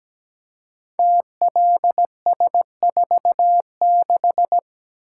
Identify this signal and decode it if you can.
Morse code, heard as a keyed tone, about 17 wpm, 700 Hz. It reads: TLS46
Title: Code Groups - 5 Letter:Numbers - 17wpm